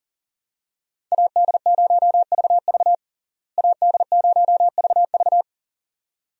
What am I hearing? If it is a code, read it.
Morse code, heard as a keyed tone, about 40 wpm, 700 Hz. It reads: AD0VV AD0VV
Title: USA Call Signs - 5 Characters - Repeated 2x - 40wpm